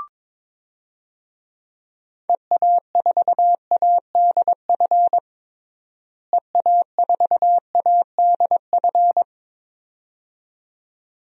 Morse code, heard as a keyed tone, about 21 wpm, 700 Hz. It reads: EA4ADF EA4ADF
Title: DXCC-100 Call Signs - 6 Characters - Repeated 2x - 22wpm